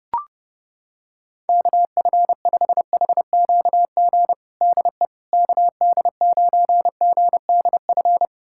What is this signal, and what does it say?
Morse code, heard as a keyed tone, about 30 wpm, 700 Hz. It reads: KF5HQG DE KD9GDF
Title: USA Callsign DE Callsign 30wpm